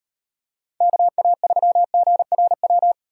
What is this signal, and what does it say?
Morse code, heard as a keyed tone, about 35 wpm, 700 Hz. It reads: KA3GRW